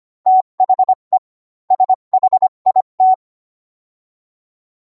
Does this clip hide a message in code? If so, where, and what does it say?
Yes: Morse code from 0.3 to 3.1 s, heard as a keyed tone, about 25 wpm, 750 Hz. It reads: THE SHIT